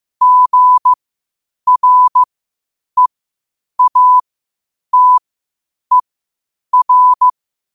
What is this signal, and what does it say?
Morse code, heard as a keyed tone, about 15 wpm, spaced as for 8 wpm, 1000 Hz. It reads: GREATER
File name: qotdfiles / 8wpm